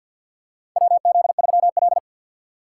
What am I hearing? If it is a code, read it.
Morse code, heard as a keyed tone, about 52 wpm, 700 Hz. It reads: WZ3L
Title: USA Call Signs - 4 Characters - 2x Character Spacing - 50wpm